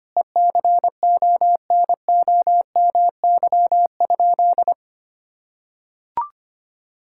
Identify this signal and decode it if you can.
Morse code, heard as a keyed tone, about 25 wpm, 700 Hz. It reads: ECONOMY?
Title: Sentences from Top 1000 Words - Review Part 6 - 25wpm